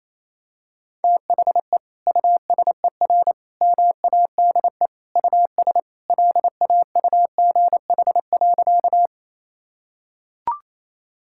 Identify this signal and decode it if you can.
Morse code, heard as a keyed tone, about 28 wpm, 700 Hz. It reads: THE USER MADE US LAUGH.
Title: Sentences from Top 1000 Words - Review Part 5 - 28wpm